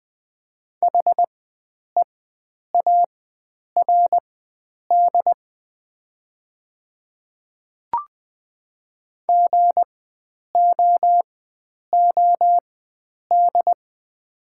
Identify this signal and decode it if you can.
Morse code, heard as a keyed tone, about 20 wpm, spaced as for 9 wpm, 700 Hz. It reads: HEARD GOOD
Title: 4x Character Spacing - Top 500 Words - 20wpm